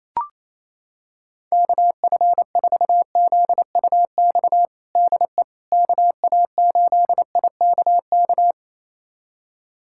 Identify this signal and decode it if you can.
Morse code, heard as a keyed tone, about 28 wpm, 700 Hz. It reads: KF4ZUX DE KA8IKK
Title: USA Callsign DE Callsign - 6 Characters - 28wpm